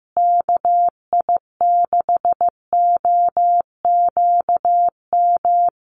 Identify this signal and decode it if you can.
Morse code, heard as a keyed tone, about 15 wpm, 700 Hz. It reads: KI6OQM
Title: USA Call Signs Repeated 3x 15wpm